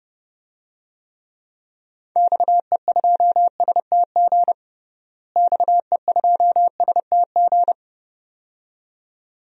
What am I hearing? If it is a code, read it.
Morse code, heard as a keyed tone, about 32 wpm, 700 Hz. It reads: XE2STG XE2STG
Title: DXCC-100 Call Signs - 6 Characters - Repeated 2x - 30wpm